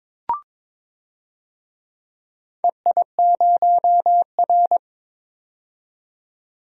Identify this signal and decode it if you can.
Morse code, heard as a keyed tone, about 21 wpm, 700 Hz. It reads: EI0R
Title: DXCC-100 Call Signs - 4 Characters - 22wpm